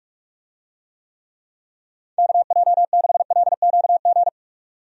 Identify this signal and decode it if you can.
Morse code, heard as a keyed tone, about 46 wpm, 700 Hz. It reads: KJ6LQG
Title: USA Call Signs - 6 Characters - 45wpm